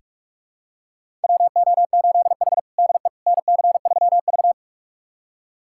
Morse code, heard as a keyed tone, about 46 wpm, 700 Hz. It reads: WO8H DE NC3V
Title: USA Callsign DE Callsign - 4 Characters - 45wpm